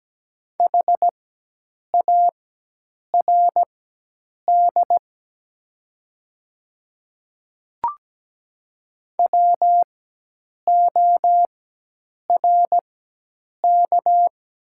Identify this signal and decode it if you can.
Morse code, heard as a keyed tone, about 17 wpm, spaced as for 8 wpm, 700 Hz. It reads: HARD WORK
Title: 4x Character Spacing - Top 300 Words - 17wpm